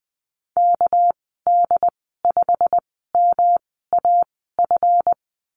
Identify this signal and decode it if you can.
Morse code, heard as a keyed tone, about 20 wpm, spaced as for 14 wpm, 700 Hz. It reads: KD5MAF
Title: 219 Lesson 35g - Number 0 - Call Signs 20wpm